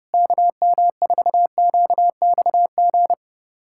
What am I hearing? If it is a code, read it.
Morse code, heard as a keyed tone, about 30 wpm, 690 Hz. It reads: KM4QXG
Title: USA Callsign DE Callsign - 6 Characters - 30wpm